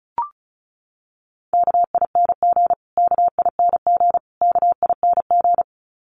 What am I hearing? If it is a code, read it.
Morse code, heard as a keyed tone, about 34 wpm, 700 Hz. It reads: KING KING KING
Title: Top 500 Words Repeated 35wpm